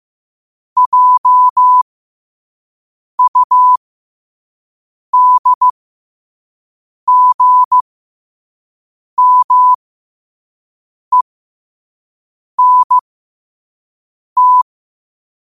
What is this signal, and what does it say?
Morse code, heard as a keyed tone, about 15 wpm, spaced as for 5 wpm, 1000 Hz. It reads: JUDGMENT
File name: qotdfiles / 5wpm